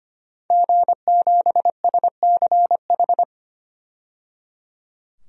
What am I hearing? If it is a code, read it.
Morse code, heard as a keyed tone, about 25 wpm, 700 Hz. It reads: G7SCH
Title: DXCC-100 Call Signs - 5 Characters - 25wpm